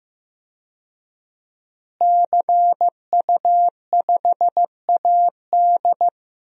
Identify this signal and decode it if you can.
Morse code, heard as a keyed tone, about 15 wpm, 700 Hz. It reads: CU5AD
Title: DXCC-100 Call Signs - 5 Characters - Repeated 2x - 15wpm